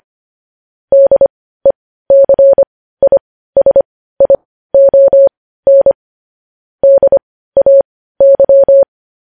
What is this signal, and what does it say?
Morse code, heard as a keyed tone, about 25 wpm, spaced as for 15 wpm, 550 Hz. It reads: DECISION DAY
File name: podcast / News headlines-1x-25x15